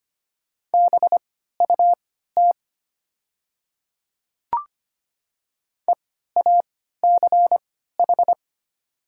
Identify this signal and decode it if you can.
Morse code, heard as a keyed tone, about 25 wpm, spaced as for 14 wpm, 700 Hz. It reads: BUT EACH
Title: 3x Character Spacing - Top 200 Words - 25wpm